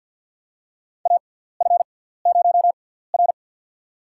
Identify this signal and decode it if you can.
Morse code, heard as a keyed tone, about 53 wpm, spaced as for 17 wpm, 700 Hz. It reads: AF0R
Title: USA Call Signs - 4 Characters - 6x Character Spacing - 50wpm